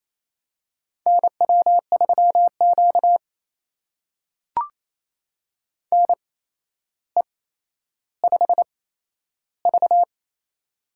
Morse code, heard as a keyed tone, about 28 wpm, spaced as for 12 wpm, 700 Hz. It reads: NW3Q NE5V
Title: USA Call Signs - 4 Characters - 8x Character Spacing - 28wpm